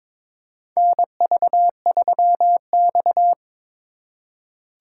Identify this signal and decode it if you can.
Morse code, heard as a keyed tone, about 22 wpm, 700 Hz. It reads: NV3X
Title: USA Call Signs - 4 Characters - 3x Character Spacing - 22wpm